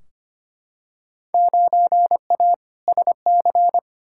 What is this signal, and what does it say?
Morse code, heard as a keyed tone, about 25 wpm, 700 Hz. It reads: 9A SC
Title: Field Day Exchanges Encore - 25wpm